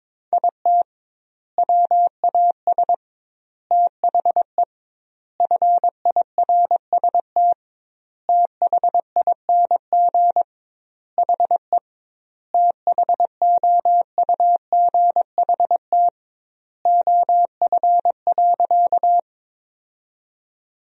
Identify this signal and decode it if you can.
Morse code, heard as a keyed tone, about 22 wpm, 700 Hz. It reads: IT WAS THE FIRST THING HE THOUGHT OF.
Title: 2x Word-Spacing - Sentences from Top 300 Words - 22wpm